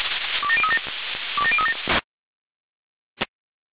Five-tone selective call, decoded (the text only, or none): 27037 27037